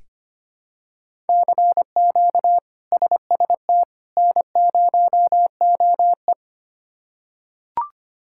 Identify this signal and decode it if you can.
Morse code, heard as a keyed tone, about 25 wpm, 700 Hz. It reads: CQ SST N0OE
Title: K1USN SST QSO Practice - 25wpm